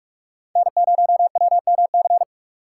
Morse code, heard as a keyed tone, about 41 wpm, 700 Hz. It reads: N0WMC
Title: USA Call Signs - 5 Characters - 2x Character Spacing - 45wpm